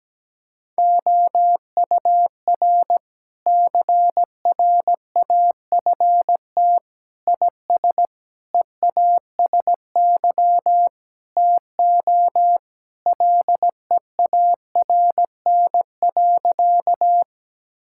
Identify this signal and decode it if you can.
Morse code, heard as a keyed tone, about 17 wpm, 700 Hz. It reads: OUR CRAFT IS EASY TO LEARN.
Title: Sentences from Top 1000 Words - Review Part 6 - 17wpm